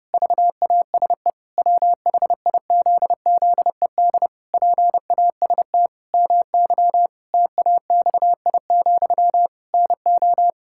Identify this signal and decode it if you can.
Morse code, heard as a keyed tone, about 32 wpm, 700 Hz. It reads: VASE WHIZZED PAST MY TAXI, NO
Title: Pangram Challenge v2 - 30wpm